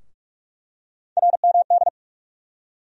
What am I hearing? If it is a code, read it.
Morse code, heard as a keyed tone, about 46 wpm, 700 Hz. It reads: RMD
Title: USA Call Sign Postfixes - 3 Letters 45wpm